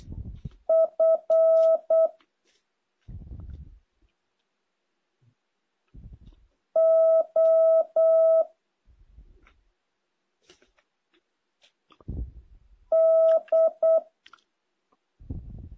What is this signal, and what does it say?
Morse code, heard as a keyed tone, about 8 wpm, 650 Hz. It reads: F O D
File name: jh8cbh / 2023cw